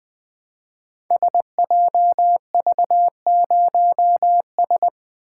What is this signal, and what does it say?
Morse code, heard as a keyed tone, about 20 wpm, 700 Hz. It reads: SJV0S
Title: Code Groups - 5 Letter:Numbers - 20wpm